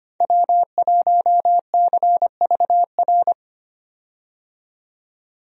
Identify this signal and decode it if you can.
Morse code, heard as a keyed tone, about 25 wpm, 700 Hz. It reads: W1CVR
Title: USA Call Signs - 5 Characters - 25wpm